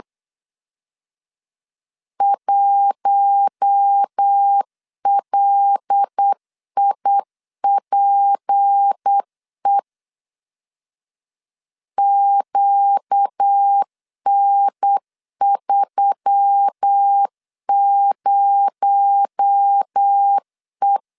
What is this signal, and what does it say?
Morse code, heard as a keyed tone, about 8 wpm, 800 Hz. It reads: 1LIPE QN30E
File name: LEZIONI / 6 PKX0 / 6 PKX0 8wpm 3341